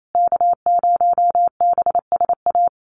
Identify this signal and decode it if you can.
Morse code, heard as a keyed tone, about 28 wpm, 690 Hz. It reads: K0BSA
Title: USA Call Signs Repeated 3x 28wpm